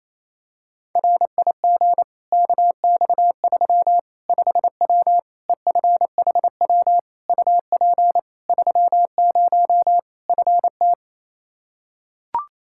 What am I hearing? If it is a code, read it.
Morse code, heard as a keyed tone, about 28 wpm, 700 Hz. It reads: RIG KX3 5W EFHW UP 30 FT
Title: Practice QSO - Exchange 2 - Info - 28wpm